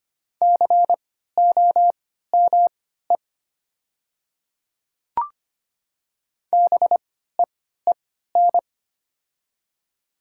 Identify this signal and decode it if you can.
Morse code, heard as a keyed tone, about 25 wpm, spaced as for 14 wpm, 700 Hz. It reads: COME BEEN
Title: 3x Character Spacing - Top 100 Words - 25wpm